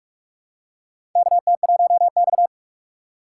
Morse code, heard as a keyed tone, about 41 wpm, 700 Hz. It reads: KT1X
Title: USA Call Signs - 4 Characters - 4x Character Spacing - 45wpm